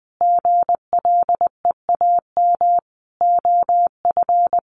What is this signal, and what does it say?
Morse code, heard as a keyed tone, about 20 wpm, 700 Hz. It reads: GLEAM OF
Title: A Sound Of Thunder 20wpm